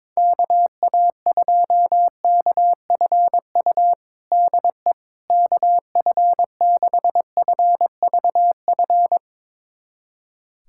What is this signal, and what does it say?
Morse code, heard as a keyed tone, about 22 wpm, 700 Hz. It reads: KA2KFU DE KF6FVF
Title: USA Callsign DE Callsign - 6 Characters - 22wpm